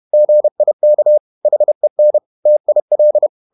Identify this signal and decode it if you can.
Morse code, heard as a keyed tone, about 31 wpm, 600 Hz. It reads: GIK HEN TIL
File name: Ferie QRQ 31wpm